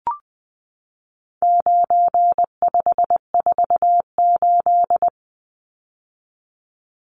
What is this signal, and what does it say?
Morse code, heard as a keyed tone, about 20 wpm, 700 Hz. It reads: 9548
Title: Numbers - 4-Digit 20wpm